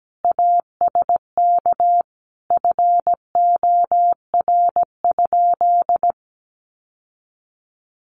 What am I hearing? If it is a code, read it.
Morse code, heard as a keyed tone, about 17 wpm, 700 Hz. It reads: ASK FOR?
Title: Sentences from Top 1400 Words 17wpm